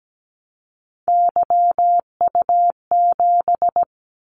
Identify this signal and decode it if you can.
Morse code, heard as a keyed tone, about 17 wpm, 700 Hz. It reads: YU7
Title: Callsign Prefixes for Top 25 DXCC - 17wpm